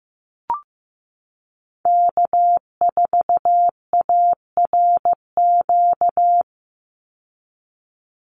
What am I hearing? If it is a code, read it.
Morse code, heard as a keyed tone, about 15 wpm, 700 Hz. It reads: K4ARQ
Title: Call Signs Repeated 3x - 15wpm